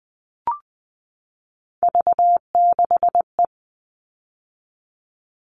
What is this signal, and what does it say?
Morse code, heard as a keyed tone, about 20 wpm, 700 Hz. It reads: V6E
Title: All Cut Numbers - 3-Digit 20wpm